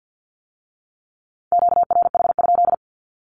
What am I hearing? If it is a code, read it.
Morse code, heard as a keyed tone, about 50 wpm, 700 Hz. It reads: QRH?
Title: Q-Codes with Spoken Meaning - 50wpm